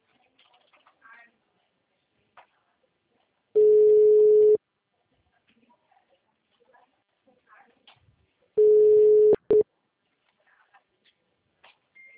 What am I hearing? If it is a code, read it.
Ringback tone (Europe, 425 Hz), with speech.